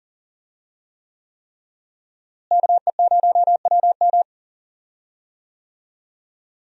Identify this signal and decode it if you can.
Morse code, heard as a keyed tone, about 41 wpm, 700 Hz. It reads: KE0WM
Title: USA Call Signs - 5 Characters - Repeated 3x - 40wpm